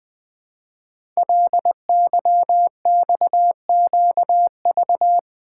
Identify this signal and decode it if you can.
Morse code, heard as a keyed tone, about 20 wpm, 700 Hz. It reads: LYXQV
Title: Code Groups - 5 Letters - 20wpm